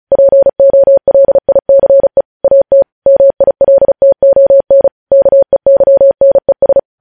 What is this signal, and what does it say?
Morse code, heard as a keyed tone, about 36 wpm, 550 Hz. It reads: POLICE AT MILTON KEYNES
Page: news headlines in morse code at 35 wpm